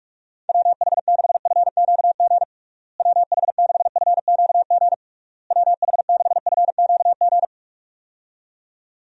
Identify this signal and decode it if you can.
Morse code, heard as a keyed tone, about 46 wpm, 700 Hz. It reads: WH6FQG WH6FQG WH6FQG
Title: USA Call Signs - 6 Characters - Repeated 3x - 45wpm